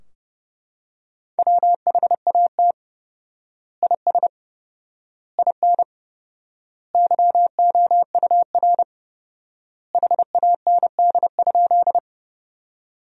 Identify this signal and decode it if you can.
Morse code, heard as a keyed tone, about 30 wpm, 700 Hz. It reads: WHAT IS IN YOUR HAND?
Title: 4x Word-Spacing - Sentences from Top 200 Words - 30wpm